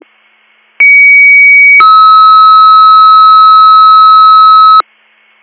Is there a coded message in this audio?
Yes: two-tone page from 0.8 to 4.8 s, heard as a one-second tone, then a three-second tone, about 2250 then 1290 Hz.